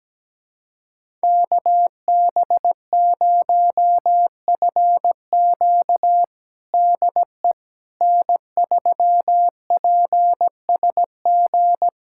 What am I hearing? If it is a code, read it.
Morse code, heard as a keyed tone, about 17 wpm, 700 Hz. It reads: KB0FQ DE N3PSG
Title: USA Callsign DE Callsign - 5 Characters - 17wpm